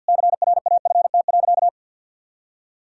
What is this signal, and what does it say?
Morse code, heard as a keyed tone, about 52 wpm, 700 Hz. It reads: CRAFT.
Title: Sentences from Top 1000 Words - Review Part 6 - 50wpm